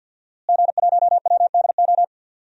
Morse code, heard as a keyed tone, about 47 wpm, 700 Hz. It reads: G1WDO
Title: DXCC-100 Call Signs - 5 Characters - Repeated 2x - 50wpm